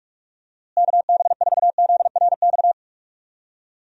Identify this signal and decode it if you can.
Morse code, heard as a keyed tone, about 42 wpm, 700 Hz. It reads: KB4ZRX
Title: USA Call Signs - 6 Characters - Repeated 2x - 45wpm